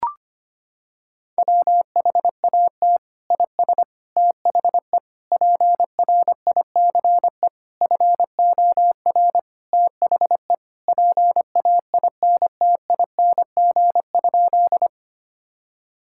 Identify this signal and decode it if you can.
Morse code, heard as a keyed tone, about 25 wpm, 700 Hz. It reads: WHAT IS THE PRICE FOR THE PAINTING?
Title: Sentences from Top 1000 Words - Review Part 6 - 25wpm